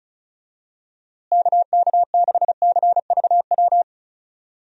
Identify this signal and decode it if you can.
Morse code, heard as a keyed tone, about 36 wpm, 700 Hz. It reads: KK6CVW
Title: USA Call Signs - 6 Characters - Repeated 3x - 35wpm